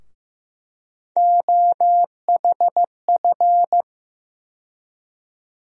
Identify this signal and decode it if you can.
Morse code, heard as a keyed tone, about 15 wpm, 700 Hz. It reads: OHF
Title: USA Call Sign Postfixes - 3 Letters 15wpm